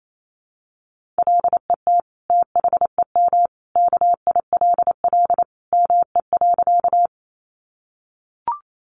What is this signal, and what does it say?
Morse code, heard as a keyed tone, about 28 wpm, 700 Hz. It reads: LET THEM KILL ME.
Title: Sentences from Top 1400 Words 28wpm